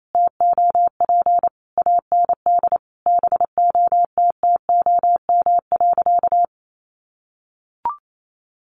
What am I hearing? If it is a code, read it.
Morse code, heard as a keyed tone, about 28 wpm, 700 Hz. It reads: TOP AND BOTTOM.
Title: Sentences from Top 1400 Words 28wpm